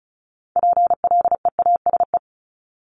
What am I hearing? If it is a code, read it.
Morse code, heard as a keyed tone, about 36 wpm, 700 Hz. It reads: PLEASE
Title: Multi-letter Phonemes 35wpm